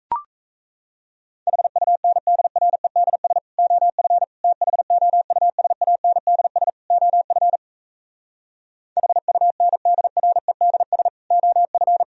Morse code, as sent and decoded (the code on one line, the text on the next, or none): .... ..- -. -.. .-. -.. ... / --- ..-. / - .... --- ..- ... .- -. -.. ... / --- ..-. / .... ..- -. -.. .-. -.. ... / --- ..-.
HUNDREDS OF THOUSANDS OF HUNDREDS OF